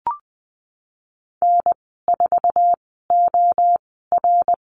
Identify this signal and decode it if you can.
Morse code, heard as a keyed tone, about 20 wpm, spaced as for 15 wpm, 700 Hz. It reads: N4OR
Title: 069 Lesson 12g - Letter d - Call Signs 20wpm